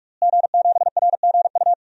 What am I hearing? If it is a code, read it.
Morse code, heard as a keyed tone, about 45 wpm, 700 Hz. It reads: G7RGU
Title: DXCC-100 Call Signs - 5 Characters - Repeated 3x - 45wpm